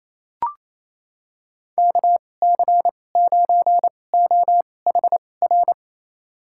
Morse code, heard as a keyed tone, about 28 wpm, spaced as for 20 wpm, 700 Hz. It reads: KC9OHR
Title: USA Call Signs - 6 Characters - 2x Character Spacing - 28wpm